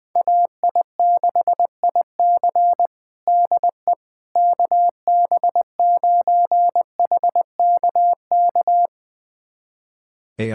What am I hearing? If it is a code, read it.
Morse code, heard as a keyed tone, about 20 wpm, 700 Hz. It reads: AI6IC DE KB9HKK
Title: USA Callsign DE Callsign 20wpm